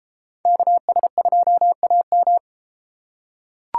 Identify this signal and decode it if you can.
Morse code, heard as a keyed tone, about 33 wpm, 700 Hz. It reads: KS2AM